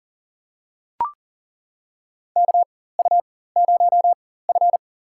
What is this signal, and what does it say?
Morse code, heard as a keyed tone, about 40 wpm, spaced as for 19 wpm, 700 Hz. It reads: KU0F
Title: USA Call Signs - 4 Characters - 4x Character Spacing - 40wpm